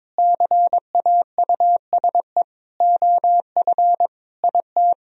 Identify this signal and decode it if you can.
Morse code, heard as a keyed tone, about 22 wpm, 700 Hz. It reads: CAUSE OF IT
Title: Sentences from Top 1000 Words - Review Part 1 - 22wpm